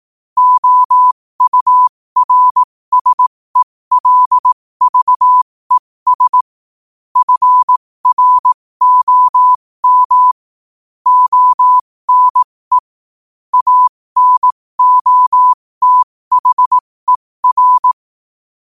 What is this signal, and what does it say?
Morse code, heard as a keyed tone, about 18 wpm, spaced as for 15 wpm, 1000 Hz. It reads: OURSELVES FROM ONE ANOTHER
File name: qotdfiles / 15wpm